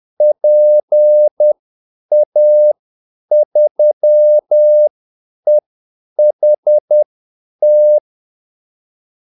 Morse code, heard as a keyed tone, about 10 wpm, spaced as for 8 wpm, 600 Hz. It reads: PA3EHT
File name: Callsigns-10-8wpm